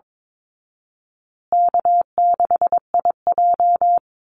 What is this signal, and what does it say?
Morse code, heard as a keyed tone, about 22 wpm, 700 Hz. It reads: K6IJ
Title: Easy Call Signs Repeated 3x - 22wpm